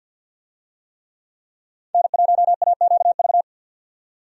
Morse code, heard as a keyed tone, about 47 wpm, 700 Hz. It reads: N1AQV
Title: USA Call Signs - 5 Characters - 50wpm